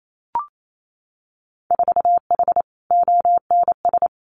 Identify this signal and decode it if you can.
Morse code, heard as a keyed tone, about 28 wpm, 700 Hz. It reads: 4H ONS
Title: Winter Field Day Exchanges Encore 28wpm